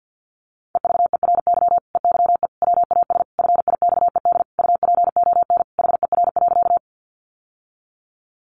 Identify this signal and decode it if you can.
Morse code, heard as a keyed tone, about 50 wpm, 700 Hz. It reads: EVERY EYE WAS FIXED UPON HER.
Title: Sentences from Top 1100 Words 50wpm